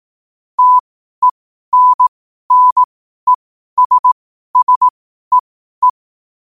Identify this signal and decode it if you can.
Morse code, heard as a keyed tone, about 18 wpm, spaced as for 12 wpm, 1000 Hz. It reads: TENNESSEE
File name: qotdfiles / 12wpm